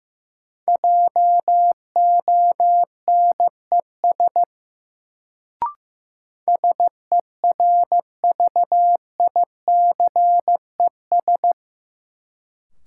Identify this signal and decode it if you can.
Morse code, heard as a keyed tone, about 15 wpm, 700 Hz. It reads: JONES SERVICES
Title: Sets of 5 Words - New Word Drill - 15wpm